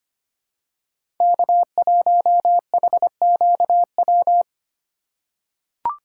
Morse code, heard as a keyed tone, about 25 wpm, 700 Hz. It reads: K1HQW
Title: USA Call Signs - 5 Characters - 4x Character Spacing - 25wpm